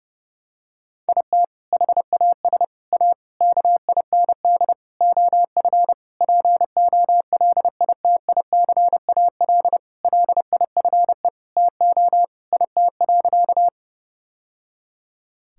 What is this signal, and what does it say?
Morse code, heard as a keyed tone, about 30 wpm, 700 Hz. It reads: IT HAS A KIND OF POLITICAL LIFE TO IT.
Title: Sentences from Top 1000 Words - Review Part 5 - 30wpm